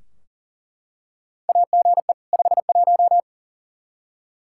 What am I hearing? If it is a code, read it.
Morse code, heard as a keyed tone, about 40 wpm, 700 Hz. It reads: AGE 51
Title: Practice QSO - Exchange 3 - More Info - 40wpm